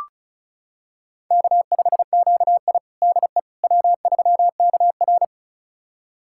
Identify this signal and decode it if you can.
Morse code, heard as a keyed tone, about 36 wpm, 700 Hz. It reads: K5QI DE W3KR
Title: USA Callsign DE Callsign - 4 Characters - 35wpm